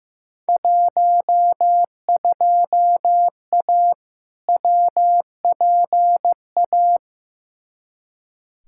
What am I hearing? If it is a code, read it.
Morse code, heard as a keyed tone, about 15 wpm, 700 Hz. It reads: 12A WPA
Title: Field Day Exchanges Encore - 15wpm